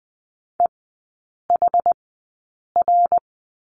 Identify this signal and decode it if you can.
Morse code, heard as a keyed tone, about 20 wpm, spaced as for 8 wpm, 700 Hz. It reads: EHR
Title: 060 Lesson 11e - Letter h - Three Characters 20wpm